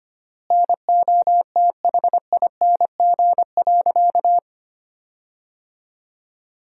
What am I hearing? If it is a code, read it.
Morse code, heard as a keyed tone, about 25 wpm, 700 Hz. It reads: NOTHING.
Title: 3x Word-Spacing - Sentences from Top 500 Words - 25wpm